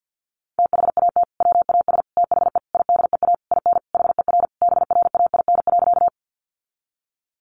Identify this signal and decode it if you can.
Morse code, heard as a keyed tone, about 50 wpm, 700 Hz. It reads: THAT WAS THE IDEA IN HER BRAIN.
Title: Sentences from Top 1100 Words 50wpm